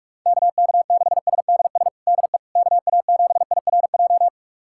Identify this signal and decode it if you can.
Morse code, heard as a keyed tone, about 46 wpm, 700 Hz. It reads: KK6SDS DE KA7IRJ
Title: USA Callsign DE Callsign - 6 Characters - 45wpm